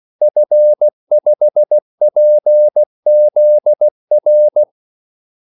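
Morse code, heard as a keyed tone, about 16 wpm, 600 Hz. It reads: F5PZR
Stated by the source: Morse call sign practice at 16 wpm.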